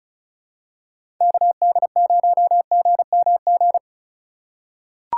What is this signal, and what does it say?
Morse code, heard as a keyed tone, about 36 wpm, 700 Hz. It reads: KD0GMG
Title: USA Call Signs - 6 Characters - 2x Character Spacing - 35wpm